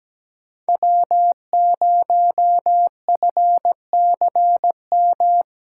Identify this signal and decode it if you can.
Morse code, heard as a keyed tone, about 17 wpm, 700 Hz. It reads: W0FCM
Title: USA Callsign DE Callsign - 5 Characters - 17wpm